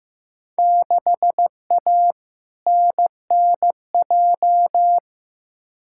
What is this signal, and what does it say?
Morse code, heard as a keyed tone, about 15 wpm, 700 Hz. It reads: 6A NNJ
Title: Field Day Exchanges Encore - 15wpm